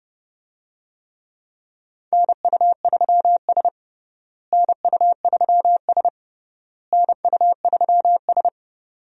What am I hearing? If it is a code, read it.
Morse code, heard as a keyed tone, about 30 wpm, 700 Hz. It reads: NU3S NU3S NU3S
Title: USA Call Signs - 4 Characters - Repeated 3x - 30wpm